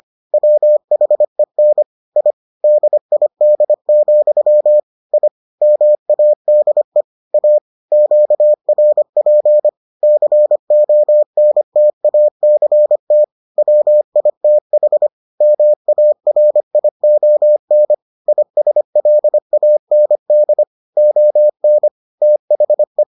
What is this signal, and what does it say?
Morse code, heard as a keyed tone, about 25 wpm, 600 Hz. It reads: WHEN I DID, I MADE A QRP CONTACT WITH MARION ISLAND ON THE